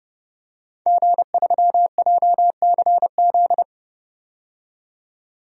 Morse code, heard as a keyed tone, about 30 wpm, 700 Hz. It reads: G3JCZ
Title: DXCC-100 Call Signs - 5 Characters - 30wpm